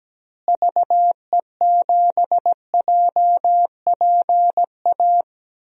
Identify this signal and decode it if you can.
Morse code, heard as a keyed tone, about 17 wpm, 700 Hz. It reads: VE7JPA
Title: DXCC-100 Call Signs - 6 Characters - Repeated 2x - 17wpm